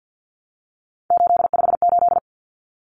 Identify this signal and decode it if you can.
Morse code, heard as a keyed tone, about 50 wpm, 700 Hz. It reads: 858
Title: Numbers - 3-Digit 50wpm